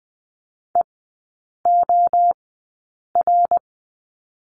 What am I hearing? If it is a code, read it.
Morse code, heard as a keyed tone, about 20 wpm, spaced as for 8 wpm, 700 Hz. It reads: EOR
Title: 114 Lesson 19e - Letter w - Three Characters 20wpm